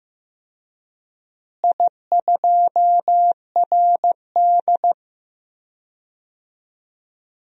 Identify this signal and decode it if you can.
Morse code, heard as a keyed tone, about 15 wpm, 700 Hz. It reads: I2RD
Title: DXCC-100 Call Signs - 4 Characters - Repeated 3x - 15wpm